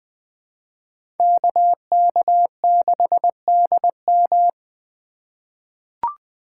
Morse code, heard as a keyed tone, about 20 wpm, 700 Hz. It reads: KK6DM
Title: USA Call Signs - 5 Characters - 4x Character Spacing - 20wpm